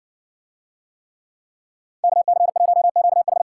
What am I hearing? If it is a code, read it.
Morse code, heard as a keyed tone, about 60 wpm, 700 Hz. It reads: KC1QH